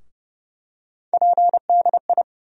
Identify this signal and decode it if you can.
Morse code, heard as a keyed tone, about 30 wpm, 700 Hz. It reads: PDI
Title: USA Call Sign Postfixes - 3 Letters 30wpm